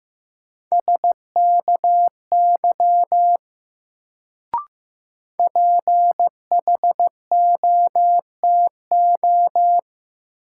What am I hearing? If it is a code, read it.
Morse code, heard as a keyed tone, about 15 wpm, 700 Hz. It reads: SKY PHOTO
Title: Sentences from Top 1200 Words - New Word Drill Alt - 15wpm